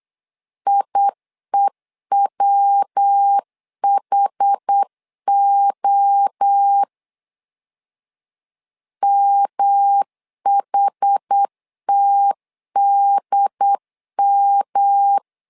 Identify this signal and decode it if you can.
Morse code, heard as a keyed tone, about 8 wpm, 800 Hz. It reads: IEWHO MHTDM
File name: LEZIONI / 4 Q1 8wpm 3035